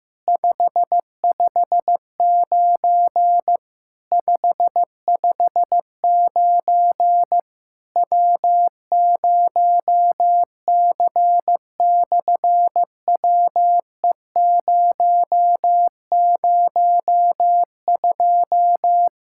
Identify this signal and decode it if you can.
Morse code, heard as a keyed tone, about 15 wpm, 700 Hz. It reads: 559 559 W0C/WE002
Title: SOTA QSO Practice 15wpm